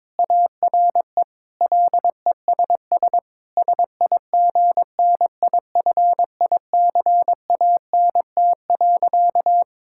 Morse code, heard as a keyed tone, about 22 wpm, 700 Hz. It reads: ARE LESS SIGNIFICANT.
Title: Sentences from Top 1000 Words - Review Part 6 - 22wpm